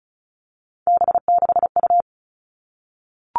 Morse code, heard as a keyed tone, about 36 wpm, 700 Hz. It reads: B6U
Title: Three Challenging Letter-Number Combinations 35wpm